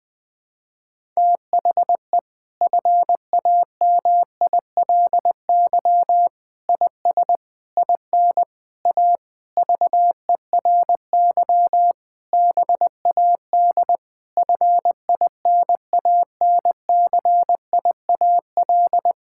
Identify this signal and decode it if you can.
Morse code, heard as a keyed tone, about 20 wpm, 700 Hz. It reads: THE FAMILY IS IN A VERY BAD FINANCIAL
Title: Sentences from Top 1000 Words - Review Part 5 - 20wpm